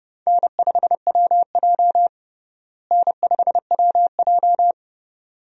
Morse code, heard as a keyed tone, about 30 wpm, 700 Hz. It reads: N5WJ N5WJ
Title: USA Call Signs - 4 Characters - Repeated 2x - 30wpm